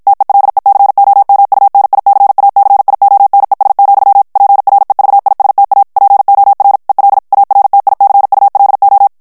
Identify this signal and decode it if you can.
Morse code, heard as a keyed tone, about 55 wpm, 800 Hz. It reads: TELECOMUNICACIONES, PREVISTA POR EL ARTICULO
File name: morse mp3 / ESPAGNOL TEXTE EN CLAIR 2 VITESSE 50 MOTS MINUTES